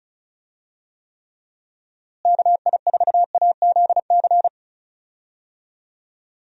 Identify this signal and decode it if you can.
Morse code, heard as a keyed tone, about 36 wpm, 700 Hz. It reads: KI4AZC
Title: USA Call Signs - 6 Characters - Repeated 3x - 35wpm